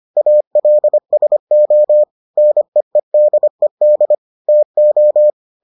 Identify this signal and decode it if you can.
Morse code, heard as a keyed tone, about 25 wpm, 600 Hz. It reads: ALSO NEEDED TO